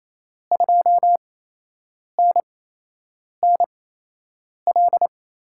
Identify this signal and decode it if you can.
Morse code, heard as a keyed tone, about 28 wpm, spaced as for 8 wpm, 700 Hz. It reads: 2NNL
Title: USA Call Signs - 6 Characters - 8x Character Spacing - 28wpm